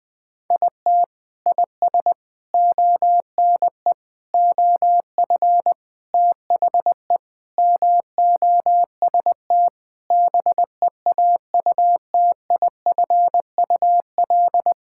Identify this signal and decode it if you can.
Morse code, heard as a keyed tone, about 21 wpm, 700 Hz. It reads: IT IS ONE OF THE MOST BEAUTIFUL
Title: Sentences with 4-Word Phrases - 20wpm